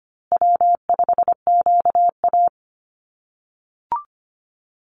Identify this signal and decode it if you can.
Morse code, heard as a keyed tone, about 25 wpm, 700 Hz. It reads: W5QA
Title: USA Call Signs Repeated 2x 25wpm